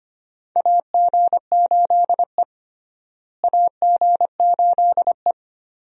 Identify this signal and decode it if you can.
Morse code, heard as a keyed tone, about 25 wpm, 700 Hz. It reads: AG8E AG8E
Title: USA Call Signs - 4 Characters - Repeated 2x - 25wpm